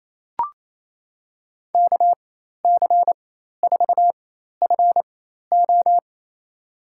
Morse code, heard as a keyed tone, about 28 wpm, spaced as for 13 wpm, 700 Hz. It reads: KC4FO
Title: USA Call Signs - 5 Characters - 4x Character Spacing - 28wpm